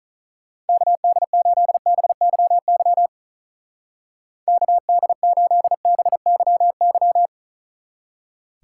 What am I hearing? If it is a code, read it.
Morse code, heard as a keyed tone, about 43 wpm, 700 Hz. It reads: KD8BYY KD8BYY
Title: Speed Racing - USA 2x3 Call Signs 35wpm